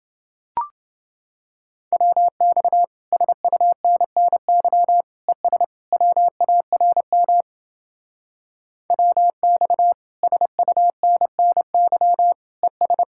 Morse code, heard as a keyed tone, about 29 wpm, 700 Hz. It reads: WX SUNNY ES WARM WX SUNNY ES